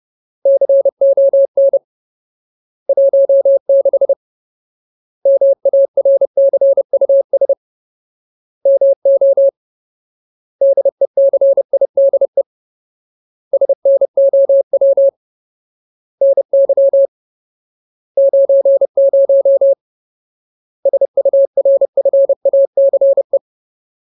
Morse code, heard as a keyed tone, about 30 wpm, 550 Hz. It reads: CON 16 MARCUS MO DECIDE SNOW NY 90 SURFACE
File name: Ultimate Word List / 30WPM Ultimate